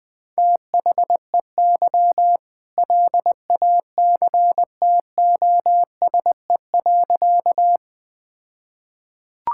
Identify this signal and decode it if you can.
Morse code, heard as a keyed tone, about 21 wpm, 700 Hz. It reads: THEY LACTOSE.
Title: All-Age Appropriate Jokes - 20wpm